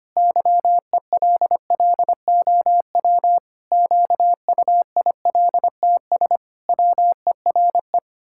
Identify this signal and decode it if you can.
Morse code, heard as a keyed tone, about 25 wpm, 700 Hz. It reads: YELLOW QUILTS WERE
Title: Pangram Challenge v2 - 25wpm